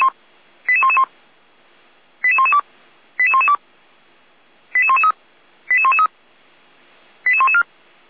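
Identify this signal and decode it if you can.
Five-tone selective call (ZVEI), heard as quick runs of five tones, tones of about 70 ms. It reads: 88111 88112 88112 88113 88113 88114
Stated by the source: ZVEI five-tone call received by SDR.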